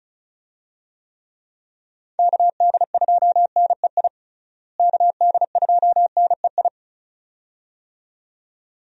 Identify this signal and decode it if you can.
Morse code, heard as a keyed tone, about 35 wpm, 700 Hz. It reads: KD2NEI KD2NEI
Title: USA Call Signs - 6 Characters - Repeated 2x - 35wpm